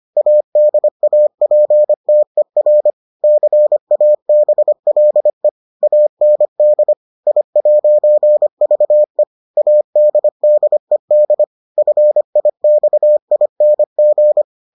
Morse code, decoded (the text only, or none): ADAPTER CABLE AND I'VE ADDED FIXING